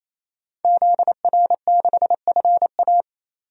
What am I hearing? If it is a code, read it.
Morse code, heard as a keyed tone, about 28 wpm, 700 Hz. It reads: ZR6FA
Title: DXCC-100 Call Signs - 5 Characters - 28wpm